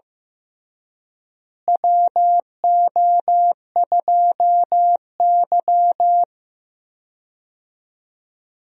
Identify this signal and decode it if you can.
Morse code, heard as a keyed tone, about 15 wpm, 700 Hz. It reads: WO2Y
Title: USA Call Signs - 4 Characters - Repeated 3x - 15wpm